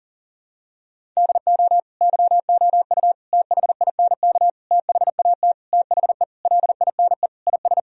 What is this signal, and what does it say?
Morse code, heard as a keyed tone, about 37 wpm, 700 Hz. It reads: DO YOU THINK THAT THE LINE IS
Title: Sentences from Top 1000 Words - Review Part 1 - 40wpm